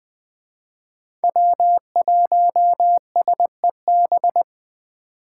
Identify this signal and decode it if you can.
Morse code, heard as a keyed tone, about 21 wpm, 700 Hz. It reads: W1SEB
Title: Speed Racing - USA 1x3 Call Signs 17wpm